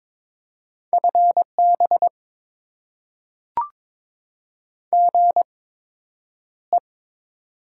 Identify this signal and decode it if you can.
Morse code, heard as a keyed tone, about 23 wpm, spaced as for 10 wpm, 700 Hz. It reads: FB GE
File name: Top 20 QSO Elements - 8x Character Spacing - 22wpm